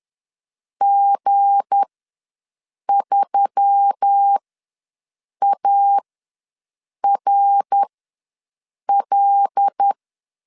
Morse code, heard as a keyed tone, about 10 wpm, spaced as for 6 wpm, 800 Hz. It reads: G3ARL
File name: LEZIONI / 7 GF9Z / 7 G ws 10wpm 3843